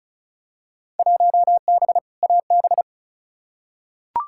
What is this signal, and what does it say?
Morse code, heard as a keyed tone, about 36 wpm, 700 Hz. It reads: 1B AB
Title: Field Day Exchanges - 35wpm